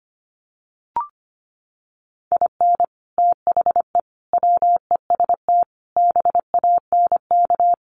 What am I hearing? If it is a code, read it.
Morse code, heard as a keyed tone, about 25 wpm, 700 Hz. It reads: IN THE WEST BANK
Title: Sets of 5 Words Encore2 25wpm